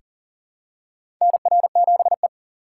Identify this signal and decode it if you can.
Morse code, heard as a keyed tone, about 37 wpm, 700 Hz. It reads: NR7E
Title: USA Call Signs - 4 Characters - 6x Character Spacing - 40wpm